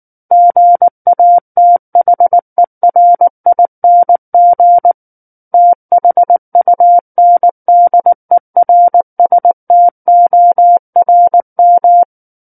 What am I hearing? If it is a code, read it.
Morse code, heard as a keyed tone, about 19 wpm, 700 Hz. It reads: GATHERING THUNDERSTORM
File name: Book One / 10-Chapter 10 - 19 WPM